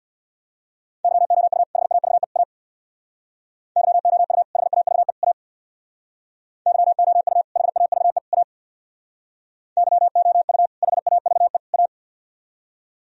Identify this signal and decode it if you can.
Morse code, heard as a keyed tone, about 64 wpm, 700 Hz. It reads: YOU HAVE A YOU HAVE A YOU HAVE A YOU HAVE A